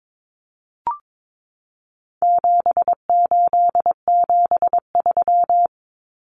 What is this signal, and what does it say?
Morse code, heard as a keyed tone, about 22 wpm, 700 Hz. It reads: 7873
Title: Common Cut Numbers - 4-Digit 22wpm